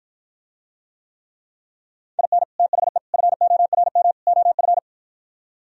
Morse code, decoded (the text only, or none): IN THE FORM OF